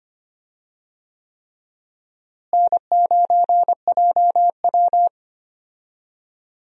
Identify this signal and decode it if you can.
Morse code, heard as a keyed tone, about 25 wpm, 700 Hz. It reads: N9JW
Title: USA Call Signs - 4 Characters - Repeated 3x - 25wpm